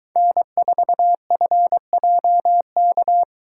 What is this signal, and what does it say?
Morse code, heard as a keyed tone, about 22 wpm, 700 Hz. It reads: N4FJK